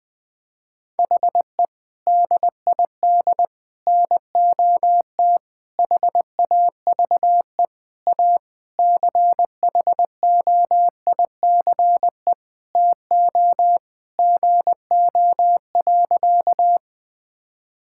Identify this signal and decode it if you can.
Morse code, heard as a keyed tone, about 19 wpm, 700 Hz. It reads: HE DID NOT HAVE A CHOICE TO GO.
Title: Sentences with 4-Word Phrases - Spoken Beginning - 20wpm